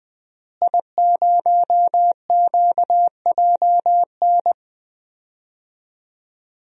Morse code, heard as a keyed tone, about 20 wpm, 700 Hz. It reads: I0QJN
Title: DXCC-100 Call Signs - 5 Characters - Repeated 3x - 20wpm